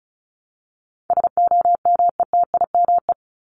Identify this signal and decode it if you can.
Morse code, heard as a keyed tone, about 34 wpm, 700 Hz. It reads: SOMETIME
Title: ES:S Suffix - Top 100 Words with Base Words 35wpm